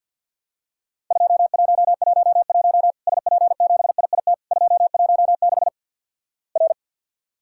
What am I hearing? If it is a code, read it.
Morse code, heard as a keyed tone, about 50 wpm, 700 Hz. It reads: 2111 SP7IIT 216 R
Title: Worked All Europe DX Contest - QTC Exchange - 50wpm